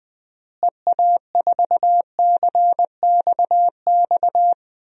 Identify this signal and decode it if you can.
Morse code, heard as a keyed tone, about 20 wpm, 700 Hz. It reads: EA4CXX
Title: DXCC-100 Call Signs - 6 Characters - Repeated 3x - 20wpm